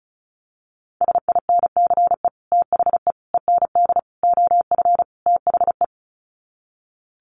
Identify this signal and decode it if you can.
Morse code, heard as a keyed tone, about 34 wpm, 700 Hz. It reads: SINCE THE END OF THE
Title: Sets of 5 Words Encore 35wpm